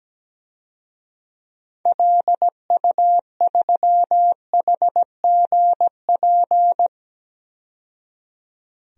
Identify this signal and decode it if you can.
Morse code, heard as a keyed tone, about 17 wpm, 700 Hz. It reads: LU3HGP
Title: DXCC-100 Call Signs - 6 Characters - 17wpm